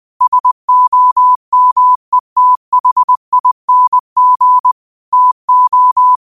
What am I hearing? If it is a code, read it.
Morse code, heard as a keyed tone, about 20 wpm, 1000 Hz. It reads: SOMETHING TO
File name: qotdfiles / 20wpm